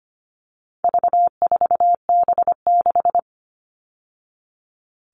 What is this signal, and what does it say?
Morse code, heard as a keyed tone, about 25 wpm, 700 Hz. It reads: V4B6
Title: All Cut Numbers - 4-Digit 25wpm